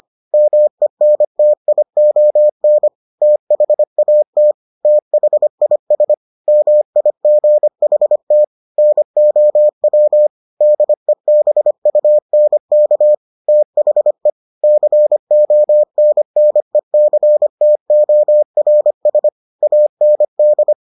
Morse code, heard as a keyed tone, about 25 wpm, 600 Hz. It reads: MENTION THAT THIS MIGHT NOW DEBUNK THE CONNECTORS AND